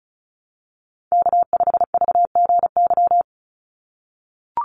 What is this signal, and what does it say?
Morse code, heard as a keyed tone, about 36 wpm, 700 Hz. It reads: K5VGY
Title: USA Call Signs 35wpm